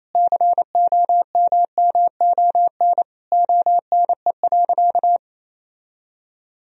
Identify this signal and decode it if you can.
Morse code, heard as a keyed tone, about 28 wpm, 700 Hz. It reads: COMMON ONE.
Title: Sentences from Top 1000 Words - Review Part 4 - 28wpm